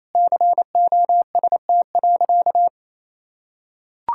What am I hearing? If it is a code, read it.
Morse code, heard as a keyed tone, about 28 wpm, 700 Hz. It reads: COST.
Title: Sentences from Top 1000 Words - Review Part 5 - 28wpm